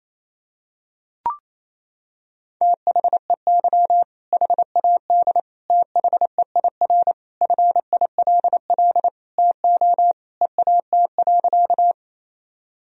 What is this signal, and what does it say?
Morse code, heard as a keyed tone, about 28 wpm, 700 Hz. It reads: THEY HAD THEIR FILL TO EAT.
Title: Sentences from Top 1000 Words - Review Part 3 - 28wpm